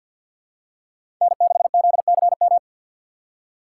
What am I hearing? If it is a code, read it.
Morse code, heard as a keyed tone, about 53 wpm, 700 Hz. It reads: N6ZCM
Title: USA Call Signs - 5 Characters - 6x Character Spacing - 50wpm